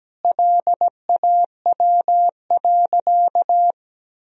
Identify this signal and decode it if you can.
Morse code, heard as a keyed tone, about 17 wpm, 700 Hz. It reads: LAW.
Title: Sentences from Top 1000 Words - Review Part 4 - 17wpm